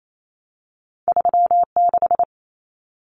Morse code, heard as a keyed tone, about 28 wpm, 700 Hz. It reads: 36
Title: Common Cut Numbers - 2-Digit 28wpm